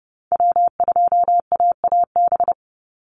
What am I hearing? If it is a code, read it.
Morse code, heard as a keyed tone, about 30 wpm, 700 Hz. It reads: W2AAB
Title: Call Signs Repeated 2x - 30wpm